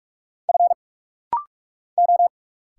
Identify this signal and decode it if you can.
Morse code, heard as a keyed tone, about 46 wpm, 700 Hz. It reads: F O
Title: Single Letters - Rapid-Fire 45wpm